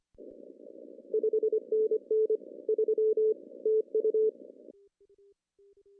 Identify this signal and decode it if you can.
Morse code, heard as a keyed tone, about 25 wpm, 440 Hz. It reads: 5NN 3 TU
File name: CW / cqww-cw-k5tr